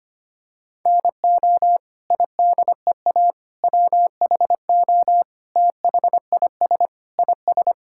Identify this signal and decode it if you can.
Morse code, heard as a keyed tone, about 25 wpm, 700 Hz. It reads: NO IDEA WHO THIS IS
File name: Sets of 9 Words - 25wpm